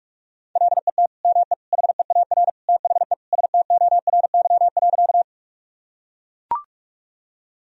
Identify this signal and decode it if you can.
Morse code, heard as a keyed tone, about 43 wpm, 700 Hz. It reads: LET ME HEAR THE STORY.
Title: Sentences from Top 1000 Words - Review Part 1 - 45wpm